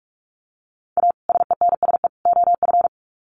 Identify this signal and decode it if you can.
Morse code, heard as a keyed tone, about 44 wpm, 700 Hz. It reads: A SENSE OF
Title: Sets of 3 Words 45wpm